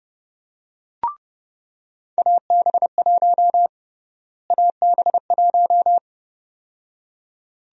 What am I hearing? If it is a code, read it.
Morse code, heard as a keyed tone, about 30 wpm, 700 Hz. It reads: AB1 AB1
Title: USA Call Sign Prefixes - Repeated 2x - 30wpm